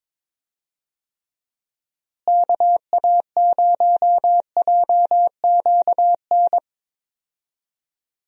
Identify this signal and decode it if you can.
Morse code, heard as a keyed tone, about 22 wpm, 700 Hz. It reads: KA0JQN
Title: USA Call Signs - 6 Characters - Repeated 3x - 22wpm